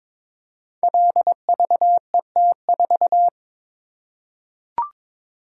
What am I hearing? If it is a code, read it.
Morse code, heard as a keyed tone, about 22 wpm, 700 Hz. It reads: LVET4
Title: Code Groups - 5 Letter:Numbers - 22wpm